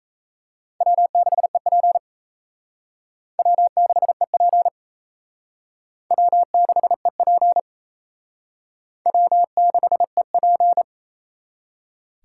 Morse code, heard as a keyed tone, about 34 wpm, 700 Hz. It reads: W6EP W6EP W6EP W6EP
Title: Speed Racing - USA 1x2 Call Signs 28wpm